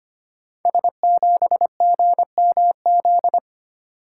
Morse code, heard as a keyed tone, about 25 wpm, 700 Hz. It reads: S7GMZ
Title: Code Groups - 5 Letter:Numbers - 25wpm